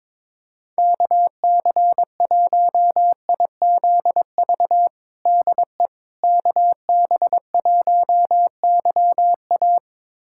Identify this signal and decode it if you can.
Morse code, heard as a keyed tone, about 22 wpm, 700 Hz. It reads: KC1IZV DE KB1YA
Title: USA Callsign DE Callsign 22wpm